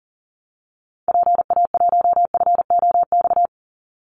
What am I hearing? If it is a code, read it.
Morse code, heard as a keyed tone, about 40 wpm, 700 Hz. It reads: PA1FOX
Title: Call Signs Repeated 2x - 40wpm